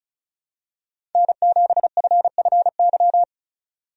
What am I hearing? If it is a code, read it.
Morse code, heard as a keyed tone, about 36 wpm, 700 Hz. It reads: N7FFY
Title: USA Call Signs - 5 Characters - 35wpm